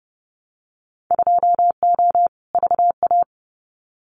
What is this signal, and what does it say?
Morse code, heard as a keyed tone, about 30 wpm, 700 Hz. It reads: 2O VA
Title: Winter Field Day Exchanges 30wpm